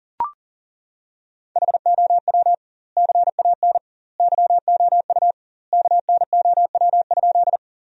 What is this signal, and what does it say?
Morse code, heard as a keyed tone, about 37 wpm, 700 Hz. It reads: HOW CAN YOU KNOW?
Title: 2x Word-Spacing - Sentences from Top 100 Words - 40wpm